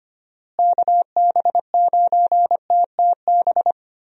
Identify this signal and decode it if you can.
Morse code, heard as a keyed tone, about 25 wpm, 700 Hz. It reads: KB9TTB
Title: USA Call Signs - 6 Characters - Repeated 2x - 25wpm